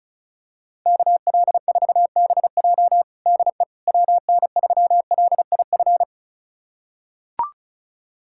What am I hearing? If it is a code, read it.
Morse code, heard as a keyed tone, about 36 wpm, 700 Hz. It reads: KL4BJ DE WN3LIF
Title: USA Callsign DE Callsign 35wpm